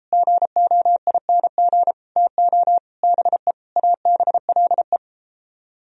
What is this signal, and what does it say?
Morse code, heard as a keyed tone, about 32 wpm, 700 Hz. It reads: GOING TO BE ABLE